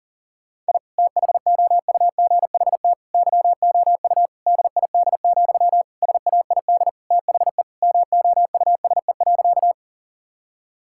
Morse code, heard as a keyed tone, about 40 wpm, 700 Hz. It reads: I THOUGHT YOU DID, SAID THE MOUSE.
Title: Sentences from Top 1000 Words - Review Part 5 - 40wpm